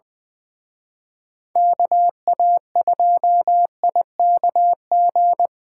Morse code, heard as a keyed tone, about 24 wpm, 700 Hz. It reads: KA2IKG
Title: Speed Racing - USA 2x3 Call Signs 20wpm